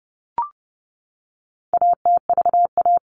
Morse code, heard as a keyed tone, about 30 wpm, 700 Hz. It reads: ATVA
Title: All Cut Numbers - 4-Digit 30wpm